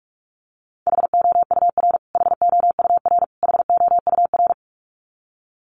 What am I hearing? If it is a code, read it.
Morse code, heard as a keyed tone, about 44 wpm, 700 Hz. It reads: HOUR HOUR HOUR
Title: Top 500 Words Repeated 45wpm